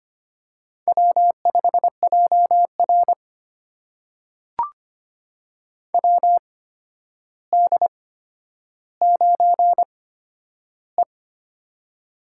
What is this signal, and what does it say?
Morse code, heard as a keyed tone, about 25 wpm, spaced as for 11 wpm, 700 Hz. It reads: W5JR WD9E
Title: USA Call Signs - 4 Characters - 8x Character Spacing - 25wpm